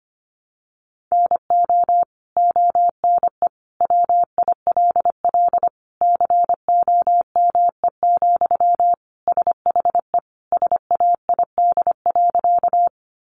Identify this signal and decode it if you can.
Morse code, heard as a keyed tone, about 25 wpm, 700 Hz. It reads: NO ONE WILL COME, SHE SAID.
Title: Sentences from Top 100 Words 25wpm